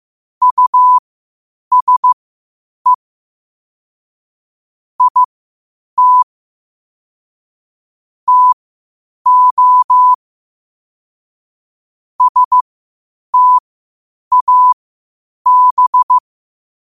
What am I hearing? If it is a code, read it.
Morse code, heard as a keyed tone, about 15 wpm, spaced as for 8 wpm, 1000 Hz. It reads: USE IT TO STAB
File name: qotdfiles / 8wpm